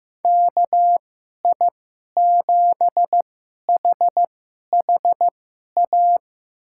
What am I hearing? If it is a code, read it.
Morse code, heard as a keyed tone, about 15 wpm, spaced as for 11 wpm, 700 Hz. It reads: KI7HHA
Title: USA Call Signs - 2x Character Spacing - 15wpm